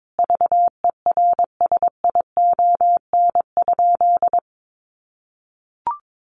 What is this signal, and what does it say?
Morse code, heard as a keyed tone, about 22 wpm, 700 Hz. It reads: VERSION?
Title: Sentences from Top 1400 Words 22wpm